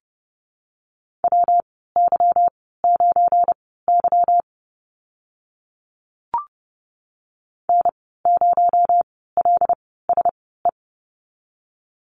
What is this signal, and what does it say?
Morse code, heard as a keyed tone, about 30 wpm, spaced as for 17 wpm, 700 Hz. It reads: WY9Y N0LSE
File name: Lesson 37g - Letter x - Call Signs 30wpm